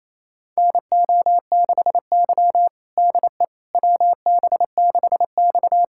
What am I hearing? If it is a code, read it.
Morse code, heard as a keyed tone, about 28 wpm, 700 Hz. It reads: NO6Y DE WB6X
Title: USA Callsign DE Callsign - 4 Characters - 28wpm